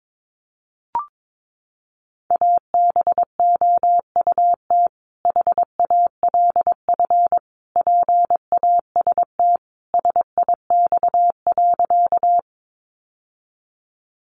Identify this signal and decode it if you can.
Morse code, heard as a keyed tone, about 22 wpm, 700 Hz. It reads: ABOUT HALF PAST SIX.
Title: Sentences from Top 600 Words 22wpm